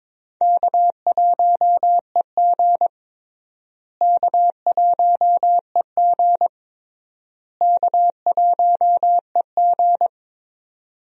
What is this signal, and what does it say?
Morse code, heard as a keyed tone, about 22 wpm, 700 Hz. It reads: K1EG K1EG K1EG
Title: USA Call Signs - 4 Characters - Repeated 3x - 22wpm